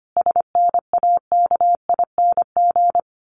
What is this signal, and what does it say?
Morse code, heard as a keyed tone, about 25 wpm, 700 Hz. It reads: SNAKING
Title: A Sound Of Thunder 25wpm